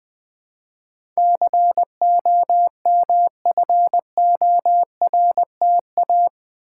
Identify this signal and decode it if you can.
Morse code, heard as a keyed tone, about 20 wpm, 700 Hz. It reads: COMFORTA
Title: Build A Word - 4 Letters At A Time - 11-Letter Words - 20wpm